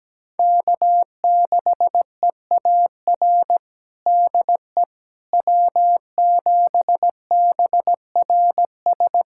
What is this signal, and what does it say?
Morse code, heard as a keyed tone, about 17 wpm, 700 Hz. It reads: K6EAR DE W7BRS
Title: USA Callsign DE Callsign - 5 Characters - 17wpm